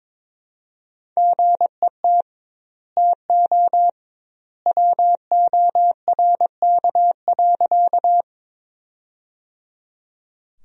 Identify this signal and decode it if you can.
Morse code, heard as a keyed tone, about 22 wpm, 700 Hz. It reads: GET TO WORK.
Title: 2x Word-Spacing - Sentences from Top 200 Words - 22wpm